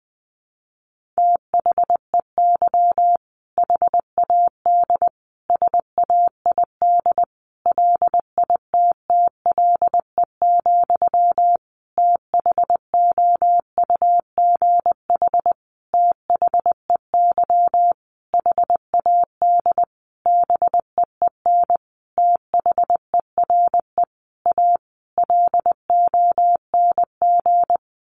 Morse code, heard as a keyed tone, about 20 wpm, 700 Hz. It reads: THEY HAD SAID LITTLE, THOUGH THEY HAD BEEN THERE A LONG
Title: Sentences from Top 400 Words 20wpm